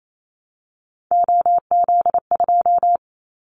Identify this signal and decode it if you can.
Morse code, heard as a keyed tone, about 28 wpm, 700 Hz. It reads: OZ2
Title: Callsign Prefixes for Top 50 DXCC - 28wpm